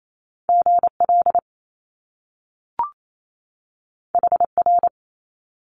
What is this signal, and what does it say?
Morse code, heard as a keyed tone, about 28 wpm, 700 Hz. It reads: GL HR
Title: Top Two Letter Combinations in Words 28wpm